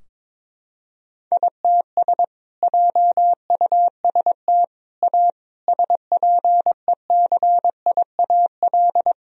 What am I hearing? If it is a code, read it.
Morse code, heard as a keyed tone, about 22 wpm, 700 Hz. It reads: ITS JUST A SPECIAL
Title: Sentences from Top 1000 Words - Review Part 4 - 22wpm